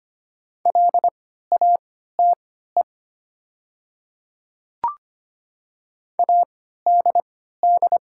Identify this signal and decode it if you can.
Morse code, heard as a keyed tone, about 25 wpm, spaced as for 14 wpm, 700 Hz. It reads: LATE ADD
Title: 3x Character Spacing - Top 300 Words - 25wpm